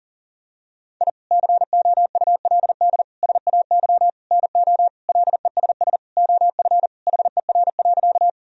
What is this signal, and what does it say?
Morse code, heard as a keyed tone, about 40 wpm, 700 Hz. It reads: I COULD SAY NO LESS OF HER.
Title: Sentences from Top 1000 Words - Review Part 2 - 40wpm